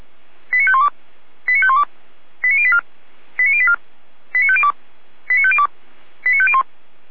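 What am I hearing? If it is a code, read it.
Five-tone selective call (ZVEI), heard as quick runs of five tones, tones of about 80 ms. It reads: 89512 89512 89084 89084 89502 89502 89501
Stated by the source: ZVEI five-tone selective call, address 89115, received on an SDR.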